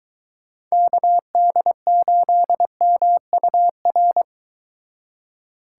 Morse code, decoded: KD8MUR

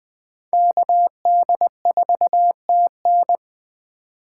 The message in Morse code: KD4TN

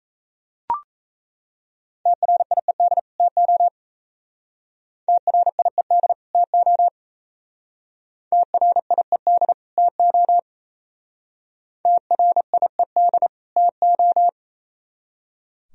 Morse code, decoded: TRIED TO TRIED TO TRIED TO TRIED TO

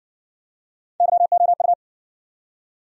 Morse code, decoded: YOU

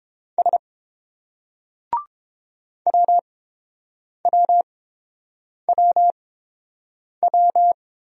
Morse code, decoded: S WWWW

